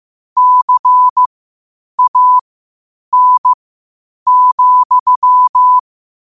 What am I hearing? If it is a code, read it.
Morse code, heard as a keyed tone, about 15 wpm, spaced as for 8 wpm, 1000 Hz. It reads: CAN,
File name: qotdfiles / 8wpm